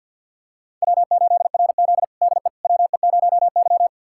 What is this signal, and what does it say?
Morse code, heard as a keyed tone, about 50 wpm, 700 Hz. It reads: W8RZ DE WE0Y
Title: USA Callsign DE Callsign - 4 Characters - 50wpm